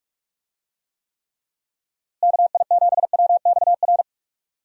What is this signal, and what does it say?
Morse code, heard as a keyed tone, about 46 wpm, 700 Hz. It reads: KI7WXR
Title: USA Call Signs - 6 Characters - Repeated 3x - 45wpm